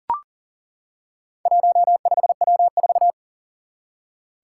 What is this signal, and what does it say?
Morse code, heard as a keyed tone, about 40 wpm, 700 Hz. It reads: 15W4
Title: Four Challenging Letter-Number Combinations 40wpm